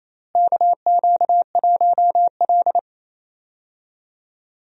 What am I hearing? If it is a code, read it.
Morse code, heard as a keyed tone, about 28 wpm, 700 Hz. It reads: KQ1L
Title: USA Call Signs - 4 Characters - Repeated 3x - 28wpm